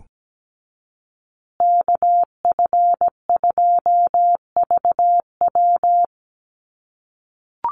Morse code, heard as a keyed tone, about 17 wpm, 700 Hz. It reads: KF2VW
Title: USA Call Signs Repeated 3x 17wpm